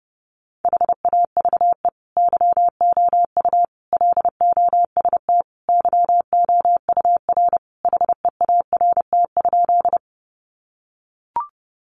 Morse code, heard as a keyed tone, about 28 wpm, 700 Hz. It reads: HAVE YOU LOST YOUR HEART?
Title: Sentences from Top 1400 Words 30wpm